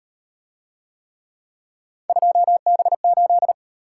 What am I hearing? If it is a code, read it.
Morse code, heard as a keyed tone, about 38 wpm, 700 Hz. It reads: 2B8